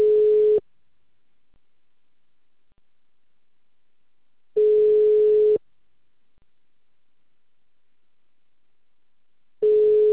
Ringback tone (Europe, 425 Hz).